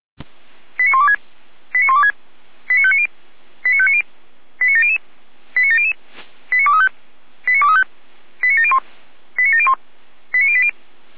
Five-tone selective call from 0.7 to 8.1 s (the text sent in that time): 89126 89126 89590 89590 89700 89700 89235 89235